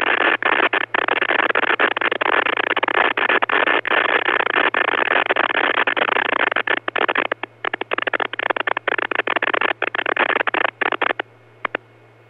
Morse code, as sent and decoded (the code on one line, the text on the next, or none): none